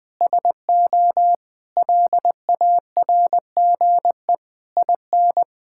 Morse code, heard as a keyed tone, about 21 wpm, 700 Hz. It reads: SO LARGE IN
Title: Sentences from Top 1000 Words - Review Part 6 - 20wpm